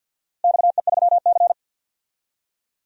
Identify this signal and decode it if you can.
Morse code, heard as a keyed tone, about 49 wpm, 700 Hz. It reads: XE3C
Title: DXCC-100 Call Signs - 4 Characters - Repeated 3x - 50wpm